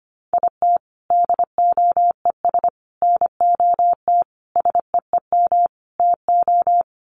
Morse code, heard as a keyed tone, about 25 wpm, 700 Hz. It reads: IT DOES NOT SEEM TO
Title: Sets of 5 Words Encore2 25wpm